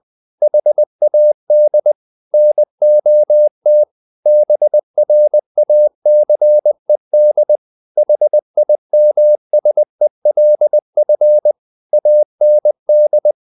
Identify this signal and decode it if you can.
Morse code, heard as a keyed tone, about 20 wpm, 600 Hz. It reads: HAD NOT BRACED HIMSELF AND
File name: the old man and the sea / the old man and the sea part 1 20wpm